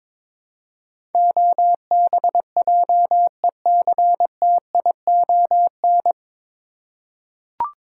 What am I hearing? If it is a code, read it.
Morse code, heard as a keyed tone, about 22 wpm, 700 Hz. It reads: OBJECTION
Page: Morse Code Ninja practice 22wpm